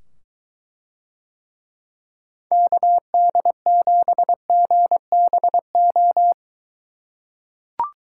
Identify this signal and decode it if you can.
Morse code, heard as a keyed tone, about 23 wpm, 700 Hz. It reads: KD7GBO